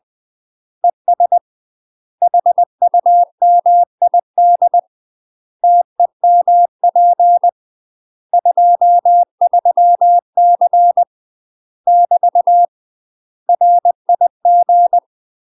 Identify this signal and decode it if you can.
Morse code, heard as a keyed tone, about 20 wpm, 700 Hz. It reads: ES HUMID TEMP 23C <BT> RIG
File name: UK MASTER QSOs TRAINING / 20WPM / UK MASTER QSOs 0013